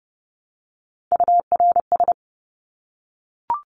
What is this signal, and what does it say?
Morse code, heard as a keyed tone, about 30 wpm, 700 Hz. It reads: URS
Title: Abbreviations Repeated 3x - 30wpm